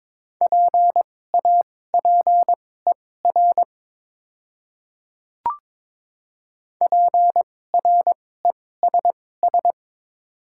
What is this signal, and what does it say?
Morse code, heard as a keyed tone, about 22 wpm, spaced as for 16 wpm, 700 Hz. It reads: PAPER PRESS
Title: 2x Character Spacing - Top 300 Words - 22wpm